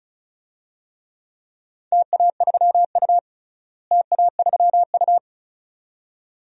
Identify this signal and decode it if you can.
Morse code, heard as a keyed tone, about 36 wpm, 700 Hz. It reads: TA3U TA3U
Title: DXCC-100 Call Signs - 4 Characters - Repeated 2x - 35wpm